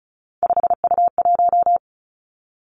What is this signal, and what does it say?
Morse code, heard as a keyed tone, about 36 wpm, 700 Hz. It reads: 5U1
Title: Three Challenging Letter-Number Combinations 35wpm